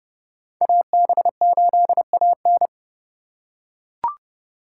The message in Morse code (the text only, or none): AB8AN